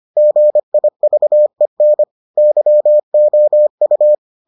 Morse code, heard as a keyed tone, about 25 wpm, 600 Hz. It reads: GIVEN YOU